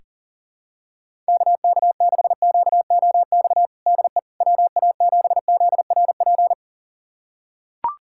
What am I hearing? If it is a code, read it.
Morse code, heard as a keyed tone, about 40 wpm, 700 Hz. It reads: KK6QOX DE WA7ZRP
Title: USA Callsign DE Callsign - 6 Characters - 40wpm